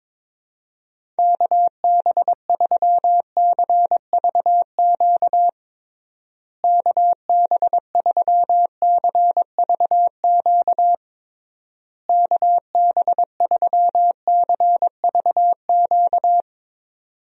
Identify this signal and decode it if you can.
Morse code, heard as a keyed tone, about 22 wpm, 700 Hz. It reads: KB3CVQ KB3CVQ KB3CVQ
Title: USA Call Signs - 6 Characters - Repeated 3x - 22wpm